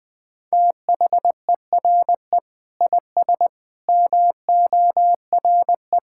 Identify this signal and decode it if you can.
Morse code, heard as a keyed tone, about 20 wpm, 700 Hz. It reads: THERE IS MORE
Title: Sentences with 9-Word Phrases - 20wpm